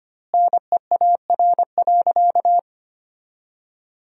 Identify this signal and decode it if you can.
Morse code, heard as a keyed tone, about 25 wpm, 700 Hz. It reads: NEAR.
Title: 3x Word-Spacing - Sentences from Top 200 Words - 25wpm